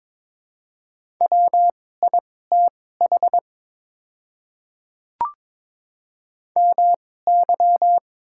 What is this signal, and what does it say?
Morse code, heard as a keyed tone, about 22 wpm, spaced as for 16 wpm, 700 Hz. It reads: WITH MY
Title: 2x Character Spacing - Top 200 Words - 22wpm